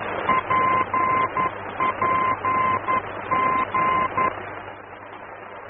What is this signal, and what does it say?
Morse code, heard as a keyed tone, about 11 wpm, 1040 Hz, with noise. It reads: PPG
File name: france / mx / PPG-F-VOR